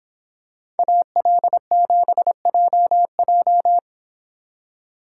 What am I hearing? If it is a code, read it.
Morse code, heard as a keyed tone, about 26 wpm, 700 Hz. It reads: AL7JJ